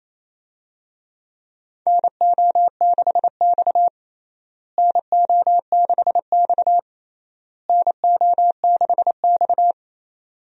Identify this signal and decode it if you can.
Morse code, heard as a keyed tone, about 28 wpm, 700 Hz. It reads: NO6X NO6X NO6X
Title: USA Call Signs - 4 Characters - Repeated 3x - 28wpm